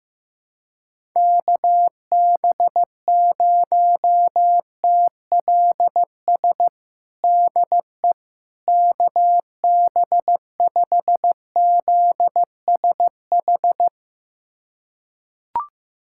Morse code, heard as a keyed tone, about 15 wpm, 700 Hz. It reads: KB0TLS DE KB5ZSH
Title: USA Callsign DE Callsign - 6 Characters - 15wpm